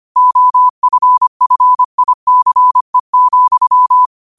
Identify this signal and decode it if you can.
Morse code, heard as a keyed tone, about 25 wpm, 1000 Hz. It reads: OFFICE,